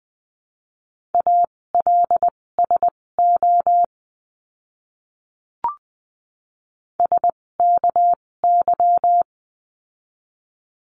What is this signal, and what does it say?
Morse code, heard as a keyed tone, about 20 wpm, spaced as for 16 wpm, 700 Hz. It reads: ALSO SKY
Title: 256 Lesson 42a - Words 12-20wpm